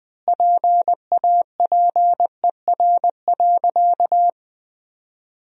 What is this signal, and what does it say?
Morse code, heard as a keyed tone, about 20 wpm, 700 Hz. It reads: PAPER.
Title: 3x Word-Spacing - Sentences from Top 300 Words - 20wpm